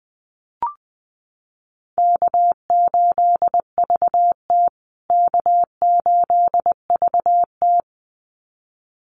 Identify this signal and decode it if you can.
Morse code, heard as a keyed tone, about 19 wpm, 700 Hz. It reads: K8VT K8VT
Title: USA Call Signs Repeated 2x 20wpm